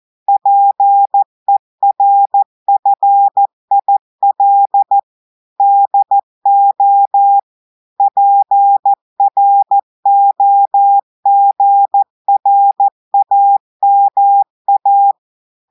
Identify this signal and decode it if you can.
Morse code, heard as a keyed tone, about 14 wpm, 800 Hz. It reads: PERFIL DO PROGRAMA
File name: audios / portugais / portugais texte en clair 2 vitesse 14 mots minutes